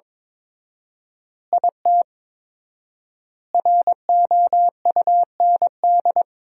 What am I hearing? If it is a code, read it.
Morse code, heard as a keyed tone, about 22 wpm, 700 Hz. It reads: IT ROUND
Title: 4x Word-Spacing - Sentences from Top 500 Words - 22wpm